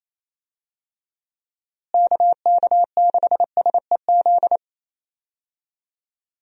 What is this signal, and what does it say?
Morse code, heard as a keyed tone, about 28 wpm, 700 Hz. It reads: KK6SEZ
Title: USA Call Signs - 6 Characters - Repeated 2x - 28wpm